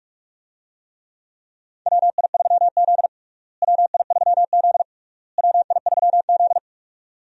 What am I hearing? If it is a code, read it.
Morse code, heard as a keyed tone, about 46 wpm, 700 Hz. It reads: WI3Z WI3Z WI3Z
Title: USA Call Signs - 4 Characters - Repeated 3x - 45wpm